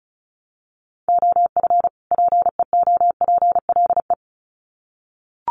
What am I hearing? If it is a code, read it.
Morse code, heard as a keyed tone, about 34 wpm, 700 Hz. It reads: OF PEOPLE
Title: Sets of 2 Words 35wpm